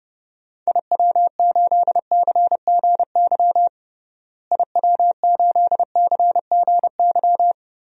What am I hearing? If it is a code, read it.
Morse code, heard as a keyed tone, about 30 wpm, 700 Hz. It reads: IW8CGY IW8CGY
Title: DXCC-100 Call Signs - 6 Characters - Repeated 3x - 30wpm